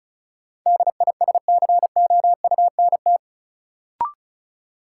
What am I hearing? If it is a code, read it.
Morse code, heard as a keyed tone, about 36 wpm, 700 Hz. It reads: DISCOUNT
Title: Sentences from Top 1200 Words - New Word Drill Alt - 35wpm